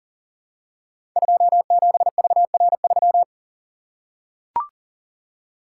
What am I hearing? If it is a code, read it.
Morse code, heard as a keyed tone, about 38 wpm, 700 Hz. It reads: 27VR3
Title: Code Groups - 5 Letter:Numbers - 40wpm